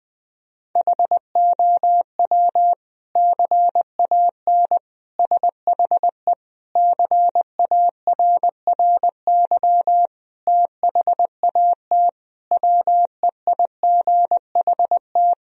Morse code, heard as a keyed tone, about 20 wpm, 700 Hz. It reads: HOW CAN SHE CARRY THAT WEIGHT
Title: Sentences from Top 1000 Words - Review Part 3 - 20wpm